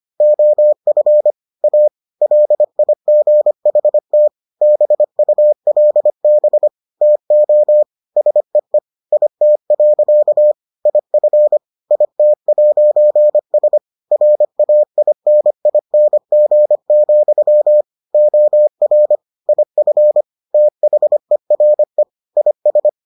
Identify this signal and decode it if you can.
Morse code, heard as a keyed tone, about 25 wpm, 600 Hz. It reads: OF A LIGHT BULB TO SEE IT. IF IT'S RAINING, OR IF THERE IS